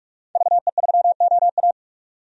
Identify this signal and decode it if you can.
Morse code, heard as a keyed tone, about 41 wpm, 700 Hz. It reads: VE3OA
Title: DXCC-100 Call Signs - 5 Characters - 45wpm